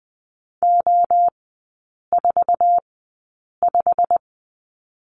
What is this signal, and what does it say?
Morse code, heard as a keyed tone, about 20 wpm, spaced as for 8 wpm, 700 Hz. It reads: O45
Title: 159 Lesson 26e - Letter g - Three Characters 20wpm